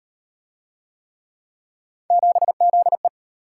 Morse code, heard as a keyed tone, about 36 wpm, 700 Hz. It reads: 7ZE